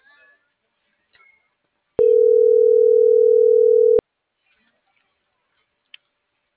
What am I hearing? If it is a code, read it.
Ringback tone (North America), 440 + 480 Hz.
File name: CENTRAL NMP / out jacqueline.castronmp